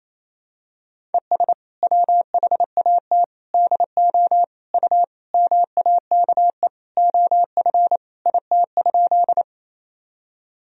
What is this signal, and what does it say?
Morse code, heard as a keyed tone, about 28 wpm, 700 Hz. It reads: ES WHAT DO U MAKE OF IT?
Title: Sentences from Top 100 Words and Abbreviations 28wpm